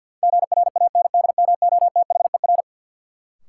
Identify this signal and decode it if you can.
Morse code, heard as a keyed tone, about 53 wpm, 700 Hz. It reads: GRANDMOTHER
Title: Sentences from Top 1200 Words - New Word Drill - 50wpm